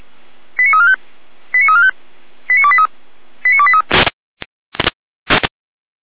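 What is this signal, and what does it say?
Five-tone selective call (ZVEI), heard as quick runs of five tones, tones of about 70 ms. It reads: 89345 89345 89283 89283
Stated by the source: ZVEI five-tone selective call received by SDR, from the Feuerwehr Orlamünde audio server.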